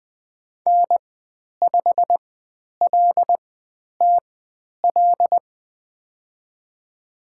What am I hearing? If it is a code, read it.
Morse code, heard as a keyed tone, about 20 wpm, spaced as for 10 wpm, 700 Hz. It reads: N5LTL
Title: 102 Lesson 17g - Letter u - Call Signs 20wpm